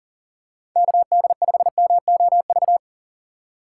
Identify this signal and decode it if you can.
Morse code, heard as a keyed tone, about 40 wpm, 700 Hz. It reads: KD5MOV
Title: USA Call Signs - 6 Characters - 6x Character Spacing - 40wpm